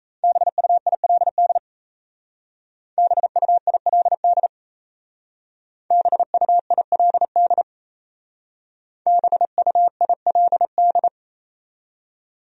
Morse code, heard as a keyed tone, about 35 wpm, 700 Hz. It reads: BUILD BUILD BUILD BUILD
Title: Speed-Racing Top 200 Words 28wpm